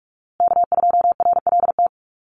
Morse code, heard as a keyed tone, about 44 wpm, 700 Hz. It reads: K2RLT
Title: USA Call Signs Repeated 2x 45wpm